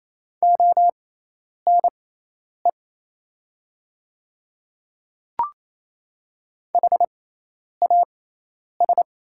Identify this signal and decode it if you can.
Morse code, heard as a keyed tone, about 28 wpm, spaced as for 10 wpm, 700 Hz. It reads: ONE HAS
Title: 6x Character Spacing - Top 100 Words - 28wpm